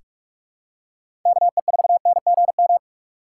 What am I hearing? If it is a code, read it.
Morse code, heard as a keyed tone, about 46 wpm, 700 Hz. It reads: KE4NGM
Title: USA Call Signs - 6 Characters - 8x Character Spacing - 45wpm